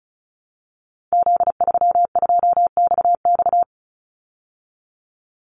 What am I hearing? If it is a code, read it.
Morse code, heard as a keyed tone, about 34 wpm, 700 Hz. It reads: Z32XX
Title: Call Signs Repeated 2x - 35wpm